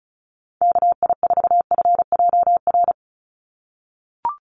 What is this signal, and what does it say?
Morse code, heard as a keyed tone, about 34 wpm, 700 Hz. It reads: KI4FJR
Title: USA Call Signs 35wpm